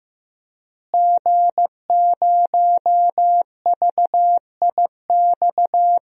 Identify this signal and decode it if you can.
Morse code, heard as a keyed tone, about 15 wpm, 700 Hz. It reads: G0VIX
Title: DXCC-100 Call Signs - 5 Characters - Repeated 3x - 15wpm